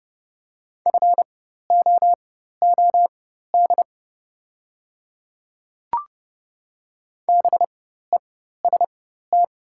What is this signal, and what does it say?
Morse code, heard as a keyed tone, about 29 wpm, spaced as for 14 wpm, 690 Hz. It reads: FOOD BEST
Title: 4x Character Spacing - Top 500 Words - 30wpm